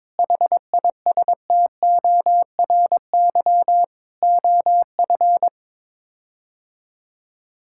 Morse code, heard as a keyed tone, about 22 wpm, 700 Hz. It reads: HISTORY OF
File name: practice / 22wpm / Sets of 8 Words - 22wpm